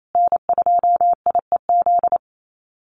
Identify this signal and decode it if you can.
Morse code, heard as a keyed tone, about 28 wpm, 700 Hz. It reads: N2IEZ
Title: USA Call Signs Repeated 3x 28wpm